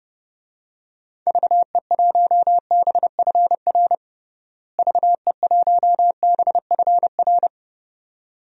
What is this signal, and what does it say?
Morse code, heard as a keyed tone, about 29 wpm, 700 Hz. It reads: VE1BFR VE1BFR
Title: DXCC-100 Call Signs - 6 Characters - Repeated 2x - 30wpm